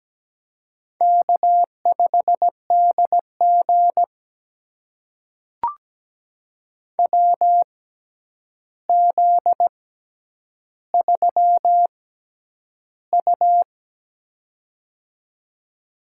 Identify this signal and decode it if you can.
Morse code, heard as a keyed tone, about 17 wpm, spaced as for 9 wpm, 700 Hz. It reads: K5DG WZ3U
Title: USA Call Signs - 4 Characters - 6x Character Spacing - 17wpm